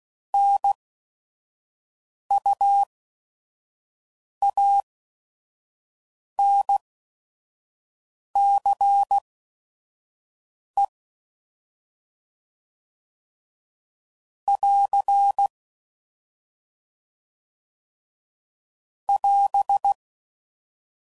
Morse code, heard as a keyed tone, about 16 wpm, spaced as for 5 wpm, 800 Hz. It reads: NUANCE <AR> <AS>
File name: Prefixes / By First Letter / Ns